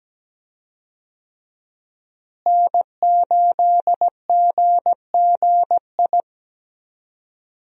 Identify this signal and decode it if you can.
Morse code, heard as a keyed tone, about 17 wpm, 700 Hz. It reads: N8GGI
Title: USA Call Signs - 5 Characters - Repeated 2x - 17wpm